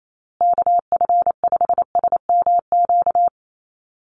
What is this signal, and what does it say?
Morse code, heard as a keyed tone, about 28 wpm, 700 Hz. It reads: KF5SMQ
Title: USA Call Signs Repeated 3x 28wpm